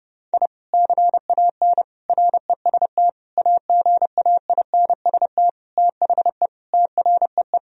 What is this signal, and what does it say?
Morse code, heard as a keyed tone, about 30 wpm, 700 Hz. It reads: I CAN REST AGAINST THE TREE
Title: Sentences from Top 1000 Words - Review Part 1 - 30wpm